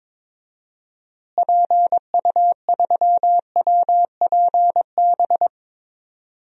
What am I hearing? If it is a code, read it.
Morse code, heard as a keyed tone, about 22 wpm, 700 Hz. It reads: PU3WPB
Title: DXCC-100 Call Signs - 6 Characters - Repeated 3x - 22wpm